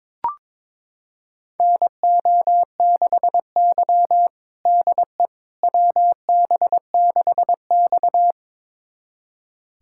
Morse code, heard as a keyed tone, about 22 wpm, 700 Hz. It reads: NO6Y DE WB6X
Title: USA Callsign DE Callsign - 4 Characters - 22wpm